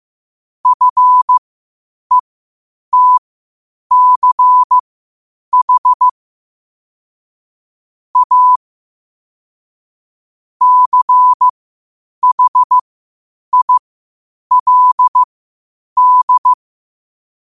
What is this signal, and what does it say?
Morse code, heard as a keyed tone, about 15 wpm, spaced as for 8 wpm, 1000 Hz. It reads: FETCH A CHILD
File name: qotdfiles / 8wpm